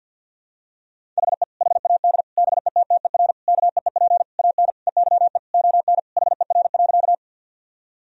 Morse code, heard as a keyed tone, about 46 wpm, 700 Hz. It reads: HE HAD BETTER KEEP AN EYE ON HER.
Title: Sentences from Top 1000 Words - Review Part 1 - 50wpm